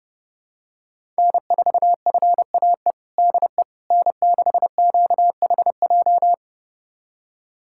Morse code, heard as a keyed tone, about 32 wpm, 700 Hz. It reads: N4FAE DE N6QHJ
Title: USA Callsign DE Callsign - 5 Characters - 30wpm